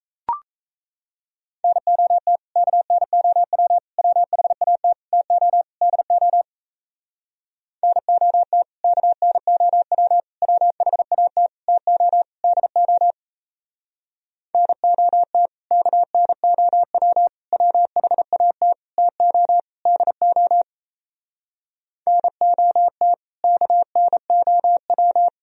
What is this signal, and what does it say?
Morse code, heard as a keyed tone, about 38 wpm, 700 Hz. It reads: NOT KNOW WHAT TO DO NOT KNOW WHAT TO DO NOT KNOW WHAT TO DO NOT KNOW